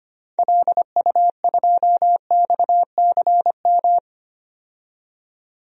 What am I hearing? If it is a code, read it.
Morse code, heard as a keyed tone, about 25 wpm, 690 Hz. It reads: LU2XCM
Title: DXCC-100 Call Signs - 6 Characters - 25wpm